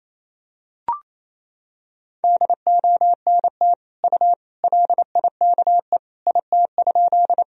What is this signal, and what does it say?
Morse code, heard as a keyed tone, about 28 wpm, 700 Hz. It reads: DONT U LIKE IT?
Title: Sentences from Top 300 Words and Abbreviations 28wpm